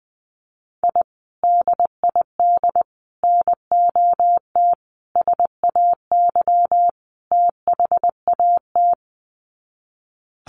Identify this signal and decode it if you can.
Morse code, heard as a keyed tone, about 20 wpm, 700 Hz. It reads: I DID NOT SAY THAT
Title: Sets of 5 Words Encore2 20wpm